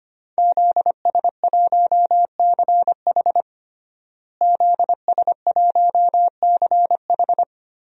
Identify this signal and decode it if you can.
Morse code, heard as a keyed tone, about 25 wpm, 700 Hz. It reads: ZS1CH ZS1CH
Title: DXCC-100 Call Signs - 5 Characters - Repeated 3x - 25wpm